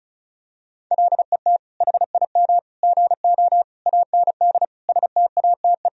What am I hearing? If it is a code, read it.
Morse code, heard as a keyed tone, about 34 wpm, 700 Hz. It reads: LET HIM GO AND STATE
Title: Sentences from Top 1000 Words - Review Part 2 - 35wpm